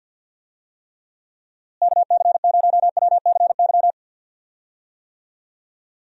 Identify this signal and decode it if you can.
Morse code, heard as a keyed tone, about 52 wpm, 700 Hz. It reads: KC0WCY
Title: USA Call Signs - 6 Characters - Repeated 3x - 50wpm